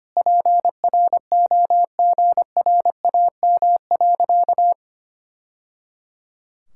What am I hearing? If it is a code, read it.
Morse code, heard as a keyed tone, about 25 wpm, 700 Hz. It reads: PROGRAM.
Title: Sentences from Top 1000 Words - Review Part 5 - 25wpm